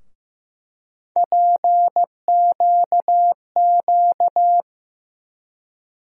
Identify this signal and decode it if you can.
Morse code, heard as a keyed tone, about 15 wpm, 700 Hz. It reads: PQQ
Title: USA Call Sign Postfixes - 3 Letters 15wpm